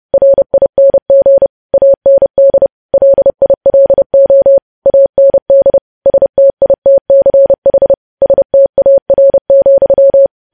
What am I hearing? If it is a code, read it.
Morse code, heard as a keyed tone, about 30 wpm, 550 Hz. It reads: RING AND LILO AND STITCH STAR,
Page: news-headlines-in-morse-code-at-30-wpm